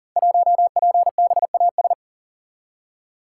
Morse code, heard as a keyed tone, about 39 wpm, 700 Hz. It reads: 1PBAS
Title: Code Groups - 5 Letter:Numbers - 40wpm